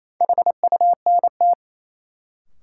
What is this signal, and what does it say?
Morse code, heard as a keyed tone, about 28 wpm, 700 Hz. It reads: HUNT
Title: Sentences from Top 1600 Words - New Word Drill - 28wpm